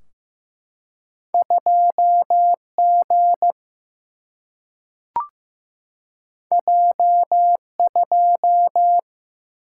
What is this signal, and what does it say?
Morse code, heard as a keyed tone, about 15 wpm, 700 Hz. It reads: 2 G J 2
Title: Single-Letter Single-Number 15wpm